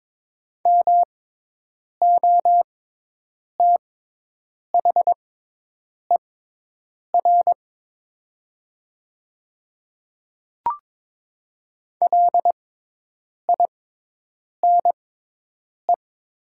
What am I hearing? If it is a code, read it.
Morse code, heard as a keyed tone, about 22 wpm, spaced as for 8 wpm, 700 Hz. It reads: MOTHER LINE
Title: 6x Character Spacing - Top 200 Words - 22wpm